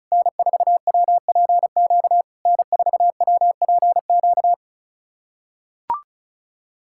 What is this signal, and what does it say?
Morse code, heard as a keyed tone, about 34 wpm, 690 Hz. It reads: N4WPQ N4WPQ
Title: SOTA QSO Practice 35wpm